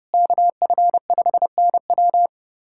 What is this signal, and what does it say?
Morse code, heard as a keyed tone, about 30 wpm, 700 Hz. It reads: KF5NW
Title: USA Callsign DE Callsign - 5 Characters - 30wpm